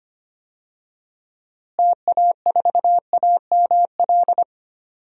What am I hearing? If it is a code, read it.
Morse code, heard as a keyed tone, about 25 wpm, 700 Hz. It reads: TA4AML
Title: DXCC-100 Call Signs - 6 Characters - Repeated 3x - 25wpm